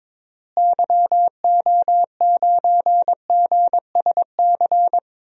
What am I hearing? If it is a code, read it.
Morse code, heard as a keyed tone, about 23 wpm, 700 Hz. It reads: YO9GSC
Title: DXCC-100 Call Signs - 6 Characters - Repeated 3x - 22wpm